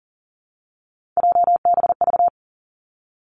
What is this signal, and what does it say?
Morse code, heard as a keyed tone, about 41 wpm, 700 Hz. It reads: JBV
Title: Three Challenging Letter-Number Combinations 40wpm